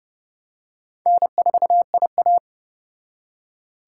Morse code, heard as a keyed tone, about 29 wpm, 700 Hz. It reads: N4IA